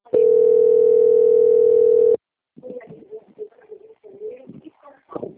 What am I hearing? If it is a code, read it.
Ringback tone (North America), 440 + 480 Hz.